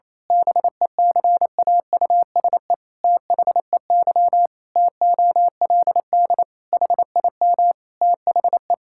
Morse code, heard as a keyed tone, about 28 wpm, 700 Hz. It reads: BECAUSE THEY TOLD HIM THE
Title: Pangram Challenge v2 - 28wpm